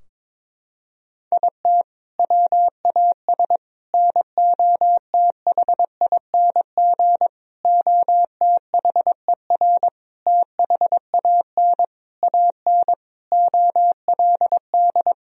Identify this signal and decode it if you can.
Morse code, heard as a keyed tone, about 22 wpm, 700 Hz. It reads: IT WAS NOTHING OTHER THAN AN OLD
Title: Sentences from Top 1000 Words - Review Part 2 - 22wpm